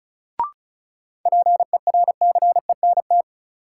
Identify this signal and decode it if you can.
Morse code, heard as a keyed tone, about 36 wpm, 700 Hz. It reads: PERCENT
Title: Sets of 2 Words - New Word Drill - 35wpm